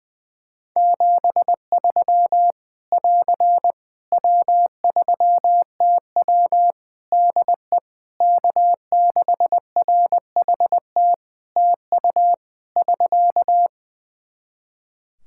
Morse code, heard as a keyed tone, about 20 wpm, 700 Hz. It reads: 73 <AR> W3TW DE K6RHT TU <SK>
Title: Practice QSO - Exchange 4 - Ending - 20wpm